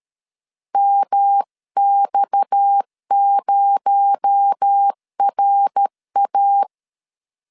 Morse code, heard as a keyed tone, about 13 wpm, 800 Hz. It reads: MX0RA
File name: LEZIONI / 6 PKX0 / 6 PKX0 12wpm 2231